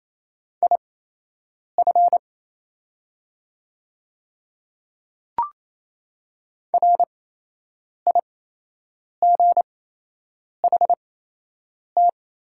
Morse code, heard as a keyed tone, about 27 wpm, spaced as for 8 wpm, 690 Hz. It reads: IF RIGHT
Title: 8x Character Spacing - Top 200 Words - 28wpm